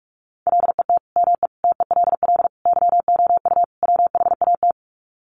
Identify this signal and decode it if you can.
Morse code, heard as a keyed tone, about 44 wpm, 700 Hz. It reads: LET ME TELL YOU WHAT
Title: Sets of 5 Words Encore 45wpm